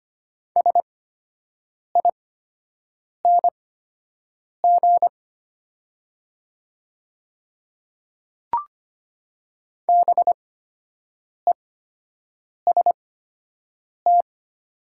Morse code, heard as a keyed tone, about 25 wpm, spaced as for 7 wpm, 700 Hz. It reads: SING BEST
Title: 8x Character Spacing - Top 500 Words - 25wpm